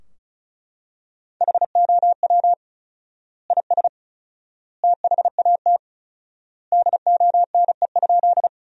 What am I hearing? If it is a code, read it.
Morse code, heard as a keyed tone, about 34 wpm, 700 Hz. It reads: HOW IS THAT DONE?
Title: 4x Word-Spacing - Sentences from Top 500 Words - 35wpm